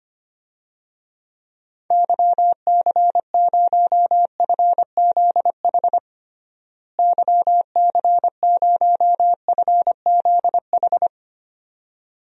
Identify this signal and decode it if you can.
Morse code, heard as a keyed tone, about 25 wpm, 700 Hz. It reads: YC0FZH YC0FZH
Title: DXCC-100 Call Signs - 6 Characters - Repeated 2x - 25wpm